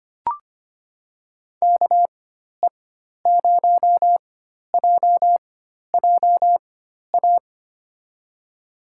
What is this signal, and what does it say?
Morse code, heard as a keyed tone, about 25 wpm, spaced as for 12 wpm, 700 Hz. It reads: KE0JJA
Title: USA Call Signs - 6 Characters - 4x Character Spacing - 25wpm